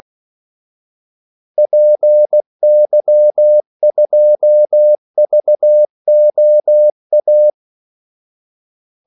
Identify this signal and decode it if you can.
Morse code, heard as a keyed tone, about 16 wpm, 600 Hz. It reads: PY2VOA
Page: Kaldesignaler 16 wpm